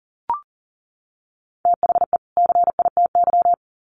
Morse code, heard as a keyed tone, about 38 wpm, 700 Hz. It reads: THE CITY
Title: Sets of 2 Words 40wpm